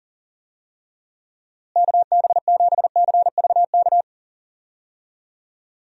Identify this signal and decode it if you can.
Morse code, heard as a keyed tone, about 37 wpm, 700 Hz. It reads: KB7CVK
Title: USA Call Signs - 6 Characters - Repeated 2x - 40wpm